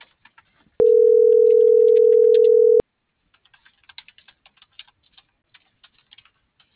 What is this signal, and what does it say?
Ringback tone (North America), 440 + 480 Hz.